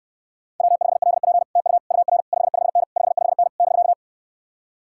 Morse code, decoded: W5LP TU UR 55N 55N <BK>

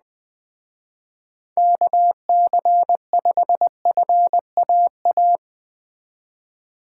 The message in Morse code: KC5FAA

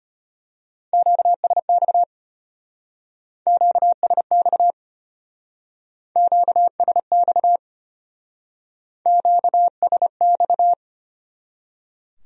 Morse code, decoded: QSX QSX QSX QSX

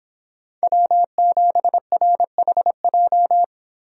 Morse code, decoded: W7RHJ

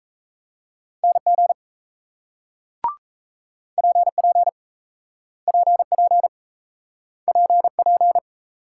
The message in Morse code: NG PP PP PP

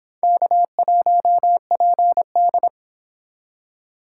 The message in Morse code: K1PD

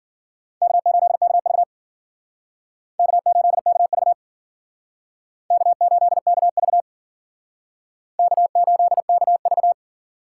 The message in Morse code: K8KV K8KV K8KV K8KV

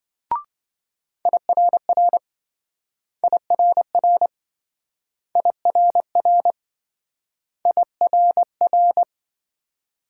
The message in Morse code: IRR IRR IRR IRR

